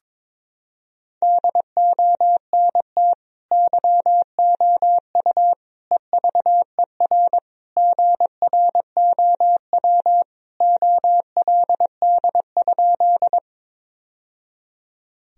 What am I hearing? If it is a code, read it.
Morse code, heard as a keyed tone, about 22 wpm, 700 Hz. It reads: DONT YOU EVER GROW OLD?
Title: Sentences from Top 1000 Words - Review Part 1 - 22wpm